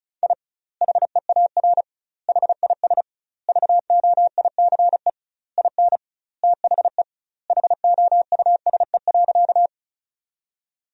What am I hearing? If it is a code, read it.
Morse code, heard as a keyed tone, about 36 wpm, 700 Hz. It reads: I HEAR HIS VOICE IN THE HOUSE.
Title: 2x Word-Spacing - Sentences from Top 500 Words - 35wpm